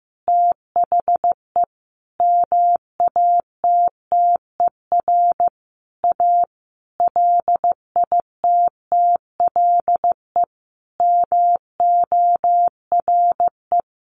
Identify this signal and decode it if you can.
Morse code, heard as a keyed tone, about 15 wpm, 700 Hz. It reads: THE MATTER A LITTLE MORE
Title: Sentences from Top 1400 Words 15wpm